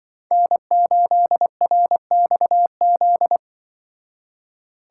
Morse code, heard as a keyed tone, about 24 wpm, 700 Hz. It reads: N8RXZ
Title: Speed Racing - USA 1x3 Call Signs 20wpm